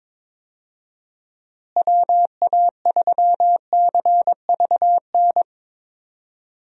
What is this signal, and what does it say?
Morse code, heard as a keyed tone, about 22 wpm, 700 Hz. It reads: WA3CVN
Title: USA Call Signs - 6 Characters - Repeated 3x - 22wpm